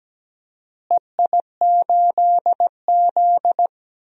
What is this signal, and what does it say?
Morse code, heard as a keyed tone, about 17 wpm, 700 Hz. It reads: EI8Z
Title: DXCC-100 Call Signs - 4 Characters - Repeated 3x - 17wpm